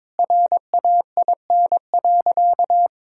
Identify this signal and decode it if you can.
Morse code, heard as a keyed tone, about 22 wpm, 700 Hz. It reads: RAIN.
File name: practice / 22wpm / Sentences with 2-Word Phrases - 22wpm